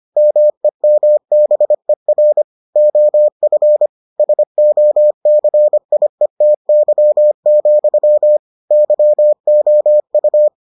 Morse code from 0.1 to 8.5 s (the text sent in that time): MEMBER OF SOCIETY,